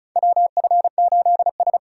Morse code, heard as a keyed tone, about 35 wpm, 700 Hz. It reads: WF8S
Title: USA Call Signs - 4 Characters - 2x Character Spacing - 35wpm